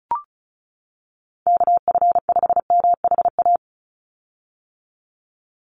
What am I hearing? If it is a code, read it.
Morse code, heard as a keyed tone, about 36 wpm, 700 Hz. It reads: KF5MHA
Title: USA Call Signs 35wpm